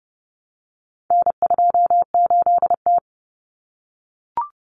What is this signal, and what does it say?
Morse code, heard as a keyed tone, about 30 wpm, 700 Hz. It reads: N28T
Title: Common Cut Numbers - 4-Digit 30wpm